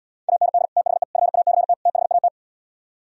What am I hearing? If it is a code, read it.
Morse code, heard as a keyed tone, about 77 wpm, 700 Hz. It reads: AND THE FACT THAT